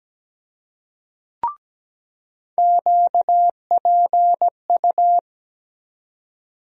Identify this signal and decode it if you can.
Morse code, heard as a keyed tone, about 17 wpm, 700 Hz. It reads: QPU
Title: USA Call Sign Postfixes - 3 Letters 17wpm